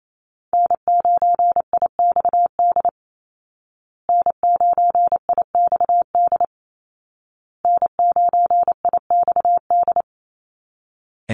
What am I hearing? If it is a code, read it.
Morse code, heard as a keyed tone, about 28 wpm, 700 Hz. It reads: N9IXD N9IXD N9IXD
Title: Call Signs Repeated 2x - 28wpm